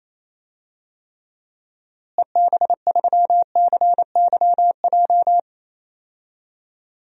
Morse code, heard as a keyed tone, about 28 wpm, 700 Hz. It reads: EB3CYJ
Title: DXCC-100 Call Signs - 6 Characters - Repeated 2x - 28wpm